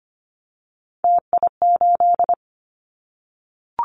Morse code, heard as a keyed tone, about 25 wpm, 700 Hz. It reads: TI8
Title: Callsign Prefixes for Top 100 DXCC - 25wpm